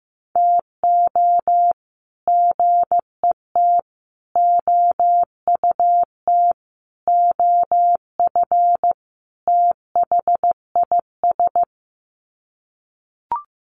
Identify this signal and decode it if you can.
Morse code, heard as a keyed tone, about 15 wpm, 700 Hz. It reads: TO GET OUT OF THIS
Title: Sets of 5 Words Encore2 15wpm